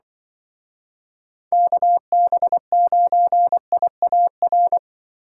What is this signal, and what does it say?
Morse code, heard as a keyed tone, about 24 wpm, 700 Hz. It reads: KB9IAR
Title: Speed Racing - USA Call Signs 20wpm